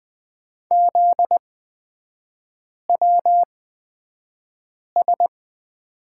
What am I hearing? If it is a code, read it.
Morse code, heard as a keyed tone, about 20 wpm, spaced as for 5 wpm, 700 Hz. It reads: ZWS
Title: 243 Lesson 39e - Letter z - Three Characters 20wpm